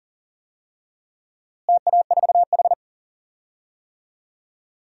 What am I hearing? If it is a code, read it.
Morse code, heard as a keyed tone, about 40 wpm, 700 Hz. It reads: TA4H
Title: DXCC-100 Call Signs - 4 Characters - Repeated 3x - 40wpm